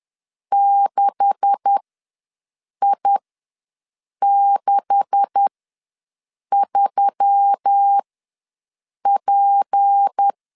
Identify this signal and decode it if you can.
Morse code, heard as a keyed tone, about 10 wpm, spaced as for 6 wpm, 800 Hz. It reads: 6I63P
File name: LEZIONI / 9 286 ws 10wpm 4032